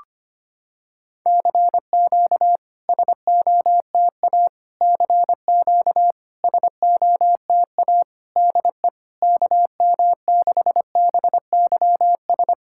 Morse code, heard as a keyed tone, about 25 wpm, 700 Hz. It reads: CQ SOTA CQ SOTA DE KM6BYS
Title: SOTA QSO Practice 25wpm